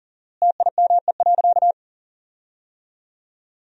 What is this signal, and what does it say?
Morse code, heard as a keyed tone, about 37 wpm, 700 Hz. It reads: TIME.
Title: 4x Word-Spacing - Sentences from Top 400 Words - 40wpm